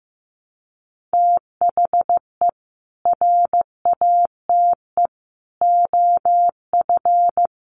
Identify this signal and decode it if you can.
Morse code, heard as a keyed tone, about 15 wpm, 700 Hz. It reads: THE RATE OF
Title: Sentences from Top 1100 Words 15wpm